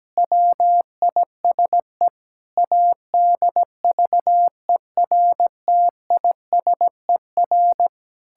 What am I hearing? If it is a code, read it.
Morse code, heard as a keyed tone, about 17 wpm, 700 Hz. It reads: WISE ADVERTISER
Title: Pangram Challenge v2 - 17wpm